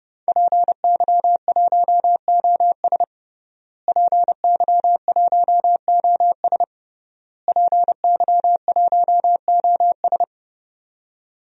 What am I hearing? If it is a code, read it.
Morse code, heard as a keyed tone, about 32 wpm, 700 Hz. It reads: PY1OS PY1OS PY1OS
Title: DXCC-100 Call Signs - 5 Characters - Repeated 3x - 30wpm